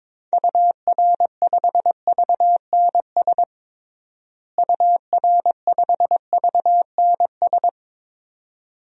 Morse code, heard as a keyed tone, about 22 wpm, 700 Hz. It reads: UR5VNS UR5VNS
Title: DXCC-100 Call Signs - 6 Characters - Repeated 2x - 22wpm